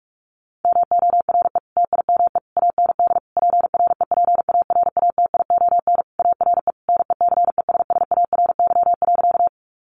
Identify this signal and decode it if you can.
Morse code, heard as a keyed tone, about 44 wpm, 700 Hz. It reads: MORE TIME AND PREPARATION ARE NECESSARY.
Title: Sentences from Top 1400 Words 45wpm